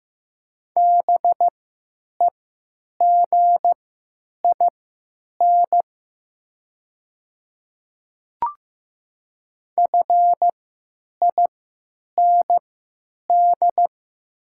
Morse code, heard as a keyed tone, about 15 wpm, spaced as for 9 wpm, 700 Hz. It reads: BEGIN FIND
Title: 3x Character Spacing - Top 300 Words - 15wpm